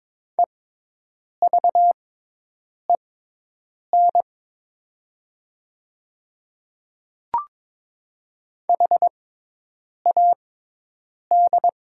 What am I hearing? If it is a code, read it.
Morse code, heard as a keyed tone, about 22 wpm, spaced as for 8 wpm, 700 Hz. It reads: EVEN HAD
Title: 6x Character Spacing - Top 200 Words - 22wpm